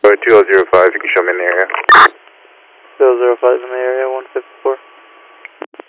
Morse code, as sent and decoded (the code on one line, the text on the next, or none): none